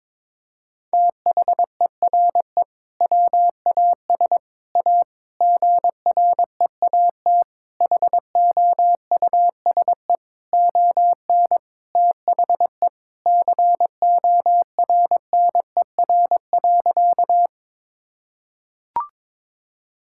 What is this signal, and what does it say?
Morse code, heard as a keyed tone, about 22 wpm, 700 Hz. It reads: THERE WAS A GREAT HOUSE ON THE CORNER.
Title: Sentences from Top 1600 Words 22wpm